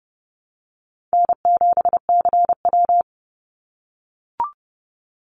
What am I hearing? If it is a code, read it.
Morse code, heard as a keyed tone, about 29 wpm, 700 Hz. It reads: N7CW
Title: NA SOTA Activator Callsigns 30wpm